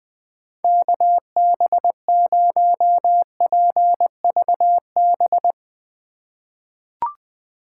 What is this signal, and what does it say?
Morse code, heard as a keyed tone, about 20 wpm, 700 Hz. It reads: KB0PVB
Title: 2023 POTA Activator Call Signs - 20wpm